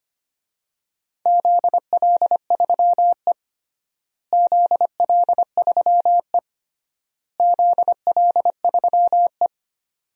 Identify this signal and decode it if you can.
Morse code, heard as a keyed tone, about 25 wpm, 700 Hz. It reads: ZL3E ZL3E ZL3E
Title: DXCC-100 Call Signs - 4 Characters - Repeated 3x - 25wpm